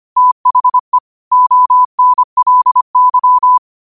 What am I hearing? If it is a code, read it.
Morse code, heard as a keyed tone, about 25 wpm, 1000 Hz. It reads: THE ONLY
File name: qotdfiles / 25wpm